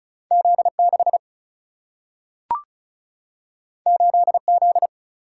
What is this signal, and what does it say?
Morse code, heard as a keyed tone, about 36 wpm, 700 Hz. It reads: Z 6 8 Z
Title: Single-Letter Single-Number 35wpm